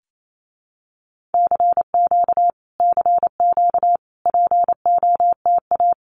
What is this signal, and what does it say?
Morse code, heard as a keyed tone, about 28 wpm, 700 Hz. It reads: CQ CQ POTA
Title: Common QSO Phrases Encore 28wpm